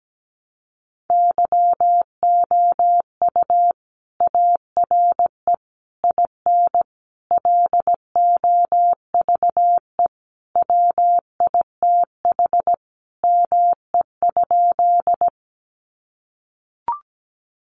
Morse code, heard as a keyed tone, about 17 wpm, 700 Hz. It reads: YOU ARE IN LOVE WITH ME?
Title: Sentences from Top 400 Words 17wpm